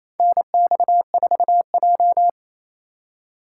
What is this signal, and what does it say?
Morse code, heard as a keyed tone, about 28 wpm, 700 Hz. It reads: NX4J
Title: USA Call Signs - 4 Characters - 3x Character Spacing - 28wpm